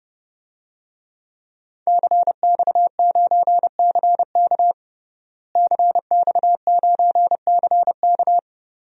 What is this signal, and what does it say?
Morse code, heard as a keyed tone, about 30 wpm, 700 Hz. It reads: CX9CK CX9CK
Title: DXCC-100 Call Signs - 5 Characters - Repeated 2x - 30wpm